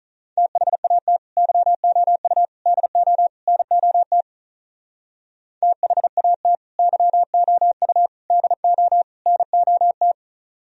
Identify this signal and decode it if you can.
Morse code, heard as a keyed tone, about 44 wpm, 700 Hz. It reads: THAT YOU DO NOT THAT YOU DO NOT